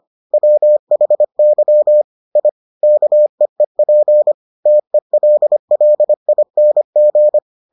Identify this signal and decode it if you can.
Morse code, heard as a keyed tone, about 25 wpm, 600 Hz. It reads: WHY I KEEP TELLING